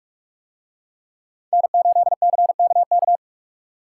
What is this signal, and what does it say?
Morse code, heard as a keyed tone, about 44 wpm, 700 Hz. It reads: N8CKK